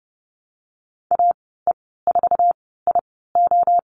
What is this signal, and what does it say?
Morse code, heard as a keyed tone, about 30 wpm, spaced as for 17 wpm, 700 Hz. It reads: AE4IO
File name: Lesson 14g - Number 2 - Call Signs 30wpm